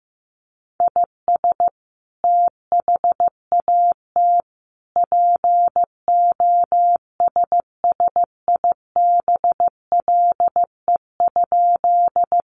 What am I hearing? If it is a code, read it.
Morse code, heard as a keyed tone, about 15 wpm, 700 Hz. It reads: IS THAT POSSIBLE?
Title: Sentences from Top 500 Words 15wpm